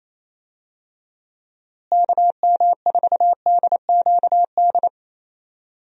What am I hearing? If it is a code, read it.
Morse code, heard as a keyed tone, about 28 wpm, 700 Hz. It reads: KM4DQD
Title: USA Call Signs - 6 Characters - Repeated 3x - 28wpm